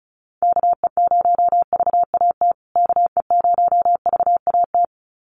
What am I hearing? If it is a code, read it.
Morse code, heard as a keyed tone, about 34 wpm, 700 Hz. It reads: KE0VAT KE0VAT
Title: USA Call Signs Repeated 2x 35wpm